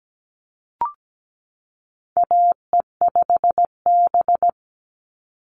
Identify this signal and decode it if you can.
Morse code, heard as a keyed tone, about 17 wpm, 700 Hz. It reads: AE5B
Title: NA SOTA Chaser Callsigns 17wpm